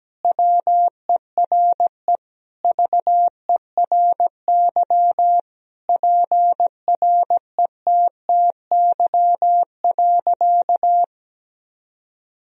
Morse code, heard as a keyed tone, about 17 wpm, 700 Hz. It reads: WERE VERY PRETTY.
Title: Sentences from Top 1000 Words - Review Part 1 - 17wpm